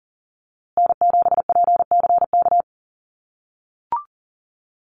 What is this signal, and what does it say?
Morse code, heard as a keyed tone, about 40 wpm, 700 Hz. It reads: N7PCK
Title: USA Call Signs Repeated 2x 40wpm